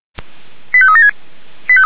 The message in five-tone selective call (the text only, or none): 85376